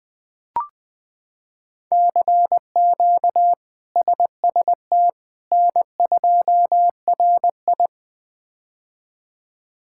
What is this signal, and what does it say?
Morse code, heard as a keyed tone, about 20 wpm, 700 Hz. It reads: CQ SST N2RI
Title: K1USN SST QSO Practice - 20wpm